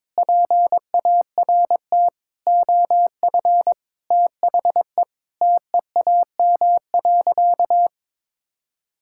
Morse code, heard as a keyed tone, about 22 wpm, 700 Hz. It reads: PART OF THE TEAM.
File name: practice / 22wpm / Sentences with 4-Word Phrases - 22wpm